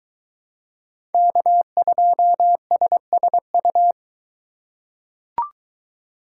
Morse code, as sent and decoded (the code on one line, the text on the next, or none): -.- ..--- ... ... ..-
K2SSU